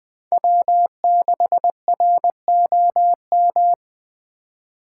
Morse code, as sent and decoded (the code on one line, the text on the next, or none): .-- -.... .-. --- --
W6ROM